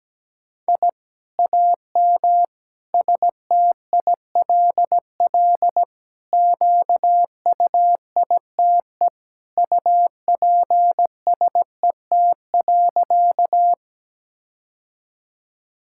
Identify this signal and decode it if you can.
Morse code, heard as a keyed tone, about 17 wpm, 700 Hz. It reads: I AM STILL QUITE UPSET.
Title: Sentences from Top 1000 Words - Review Part 5 - 17wpm